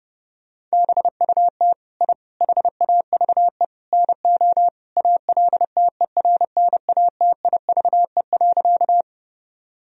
Morse code, heard as a keyed tone, about 32 wpm, 700 Hz. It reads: BUT I HAVE NO ALTERNATIVE.
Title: Sentences from Top 1000 Words - Review Part 4 - 30wpm